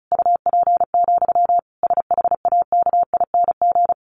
Morse code, heard as a keyed tone, about 34 wpm, 700 Hz. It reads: UP, SHAKING
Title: A Sound Of Thunder 35wpm